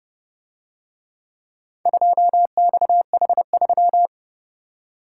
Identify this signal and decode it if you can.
Morse code, heard as a keyed tone, about 29 wpm, 700 Hz. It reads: 2XH3